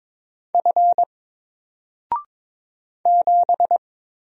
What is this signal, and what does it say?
Morse code, heard as a keyed tone, about 22 wpm, 700 Hz. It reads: F 7
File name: Single-Letter-Number-22wpm